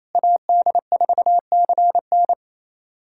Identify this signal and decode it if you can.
Morse code, heard as a keyed tone, about 28 wpm, 700 Hz. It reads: AD4CN
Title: USA Callsign DE Callsign - 5 Characters - 28wpm